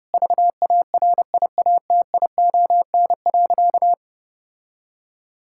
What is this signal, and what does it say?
Morse code, heard as a keyed tone, about 30 wpm, 700 Hz. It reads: VARIATION.
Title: Sentences from Top 1600 Words 30wpm